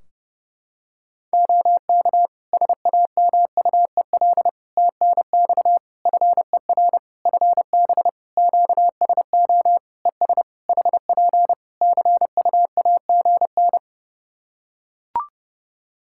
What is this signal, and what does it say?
Morse code, heard as a keyed tone, about 30 wpm, 700 Hz. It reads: OK SAMUEL TNX FER FB QSO ES HP CUAGN
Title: K1USN SST QSO Practice 30wpm